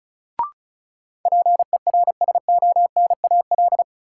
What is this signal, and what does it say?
Morse code, heard as a keyed tone, about 34 wpm, 700 Hz. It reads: PERSONAL
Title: Sentences from Top 700 Words - New Word Drill - 35wpm